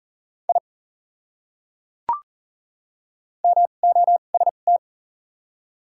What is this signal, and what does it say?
Morse code, heard as a keyed tone, about 40 wpm, spaced as for 29 wpm, 700 Hz. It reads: I MOST
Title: 2x Character Spacing - Top 100 Words - 40wpm